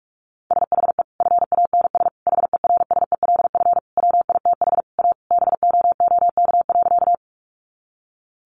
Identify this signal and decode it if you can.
Morse code, heard as a keyed tone, about 44 wpm, 700 Hz. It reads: SHE FANS HERSELF WITH A BOOK.
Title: Easy Phrases 45wpm